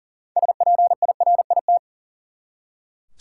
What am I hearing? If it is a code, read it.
Morse code, heard as a keyed tone, about 40 wpm, 700 Hz. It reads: SPIRIT
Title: Six Letter Words - 40wpm